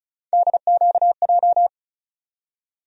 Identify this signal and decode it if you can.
Morse code, heard as a keyed tone, about 34 wpm, 700 Hz. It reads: DQJ
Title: USA Call Sign Postfixes - 3 Letters 35wpm